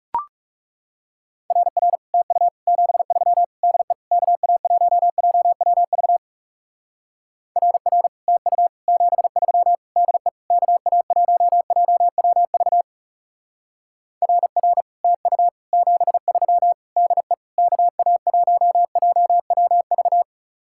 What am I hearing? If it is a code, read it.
Morse code, heard as a keyed tone, about 37 wpm, 700 Hz. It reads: RR TU 73 DE KA1JWV RR TU 73 DE KA1JWV RR TU 73 DE KA1JWV
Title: Speed-Racing Common QSO Phrases - 30wpm